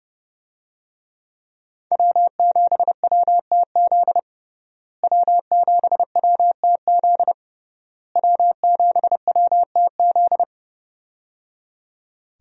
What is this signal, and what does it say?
Morse code, heard as a keyed tone, about 30 wpm, 700 Hz. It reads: W7WTZ W7WTZ W7WTZ
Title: USA Call Signs - 5 Characters - Repeated 3x - 30wpm